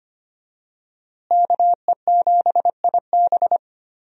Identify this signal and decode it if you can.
Morse code, heard as a keyed tone, about 25 wpm, 700 Hz. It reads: KE7IB
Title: USA Call Signs - 5 Characters - Repeated 3x - 25wpm